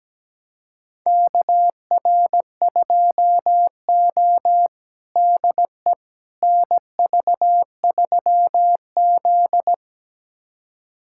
Morse code, heard as a keyed tone, about 17 wpm, 700 Hz. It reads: KR2O DE NV3Z
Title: USA Callsign DE Callsign - 4 Characters - 17wpm